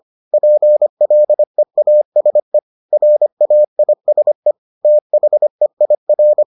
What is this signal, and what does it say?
Morse code, heard as a keyed tone, about 25 wpm, 600 Hz. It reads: PLEASE RAISE THEIR